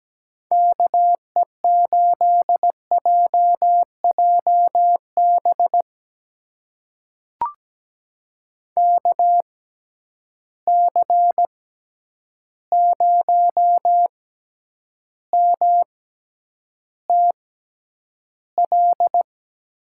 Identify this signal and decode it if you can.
Morse code, heard as a keyed tone, about 17 wpm, spaced as for 9 wpm, 700 Hz. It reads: KE8JJB KC0MTL
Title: USA Call Signs - 6 Characters - 6x Character Spacing - 17wpm